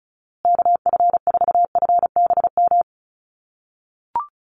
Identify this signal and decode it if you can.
Morse code, heard as a keyed tone, about 34 wpm, 700 Hz. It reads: KF4FBM
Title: USA Call Signs Repeated 2x 35wpm